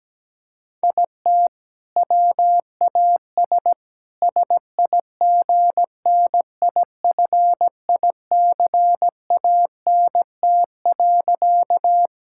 Morse code, heard as a keyed tone, about 17 wpm, 700 Hz. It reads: IT WAS SIGNIFICANT.
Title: Sentences from Top 1000 Words - Review Part 6 - 17wpm